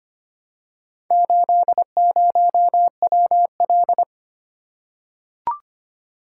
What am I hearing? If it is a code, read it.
Morse code, heard as a keyed tone, about 25 wpm, 690 Hz. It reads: 80WL
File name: practice / 25wpm / Four Letter-Number Combinations - 25wpm